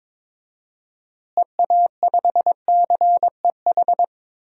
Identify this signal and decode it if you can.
Morse code, heard as a keyed tone, about 22 wpm, 700 Hz. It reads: EA5CEH
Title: DXCC-100 Call Signs - 6 Characters - Repeated 2x - 22wpm